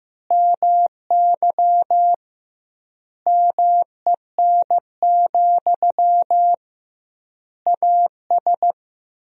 Morse code, heard as a keyed tone, about 15 wpm, 700 Hz. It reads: MY MEN, AS
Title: 2x Word-Spacing - Sentences from Top 500 Words - 15wpm